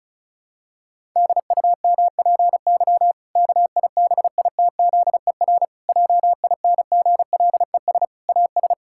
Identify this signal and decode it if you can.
Morse code, heard as a keyed tone, about 36 wpm, 700 Hz. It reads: DUMPY KIBITZER JINGLES AS
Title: Pangram Challenge v2 - 35wpm